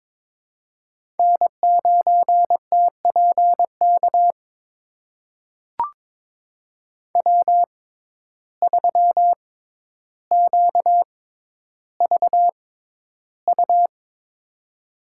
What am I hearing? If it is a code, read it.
Morse code, heard as a keyed tone, about 22 wpm, spaced as for 11 wpm, 700 Hz. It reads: N9TPK W3QVU
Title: USA Call Signs - 5 Characters - 6x Character Spacing - 22wpm